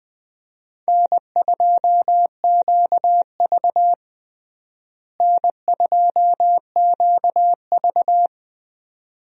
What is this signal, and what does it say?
Morse code, heard as a keyed tone, about 20 wpm, 700 Hz. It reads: N2QV N2QV
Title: USA Call Signs - 4 Characters - Repeated 2x - 20wpm